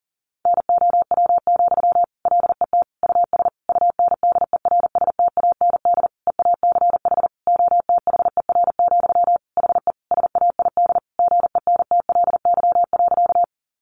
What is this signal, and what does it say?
Morse code, heard as a keyed tone, about 40 wpm, 700 Hz. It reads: NOW, LET US UNDERSTAND EACH OTHER, HE SAID GENTLY.
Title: Sentences from Top 1500 Words 40wpm